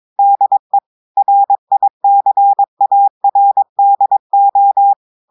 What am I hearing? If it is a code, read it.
Morse code, heard as a keyed tone, about 22 wpm, 800 Hz. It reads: DE RICARDO
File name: audios / portugais / portugais texte en clair 1 vitesse 22 mots minutes